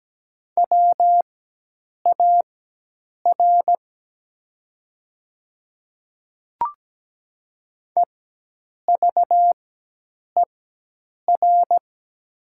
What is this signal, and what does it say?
Morse code, heard as a keyed tone, about 17 wpm, spaced as for 8 wpm, 700 Hz. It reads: WAR EVER
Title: 4x Character Spacing - Top 500 Words - 17wpm